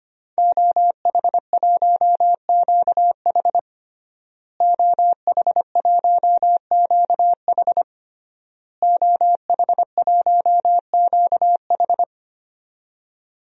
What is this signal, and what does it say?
Morse code, heard as a keyed tone, about 25 wpm, 700 Hz. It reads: OH1QH OH1QH OH1QH
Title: DXCC-100 Call Signs - 5 Characters - Repeated 3x - 25wpm